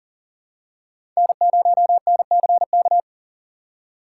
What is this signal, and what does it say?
Morse code, heard as a keyed tone, about 40 wpm, 700 Hz. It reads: N0NCK
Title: USA Call Signs - 6x Character Spacing - 40wpm